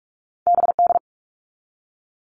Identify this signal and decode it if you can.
Morse code, heard as a keyed tone, about 44 wpm, 700 Hz. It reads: BD
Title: All Cut Numbers - 2-Digit 45wpm